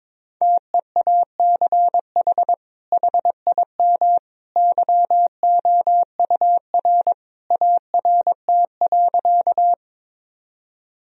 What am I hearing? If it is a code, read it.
Morse code, heard as a keyed tone, about 23 wpm, 700 Hz. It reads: TEACH HIM YOUR ART.
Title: Sentences from Top 1000 Words - Review Part 3 - 22wpm